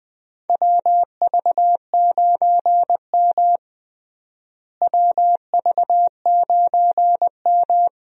Morse code, heard as a keyed tone, about 21 wpm, 700 Hz. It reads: WV9M WV9M
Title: USA Call Signs - 4 Characters - Repeated 2x - 20wpm